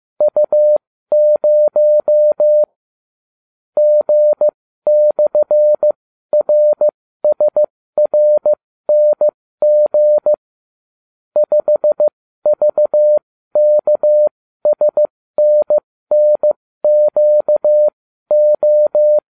Morse code, heard as a keyed tone, about 15 wpm, spaced as for 13 wpm, 600 Hz. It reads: U0 G/RSRNG 5VKSNNQO